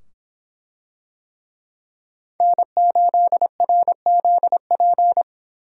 Morse code, heard as a keyed tone, about 24 wpm, 700 Hz. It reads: N8RZP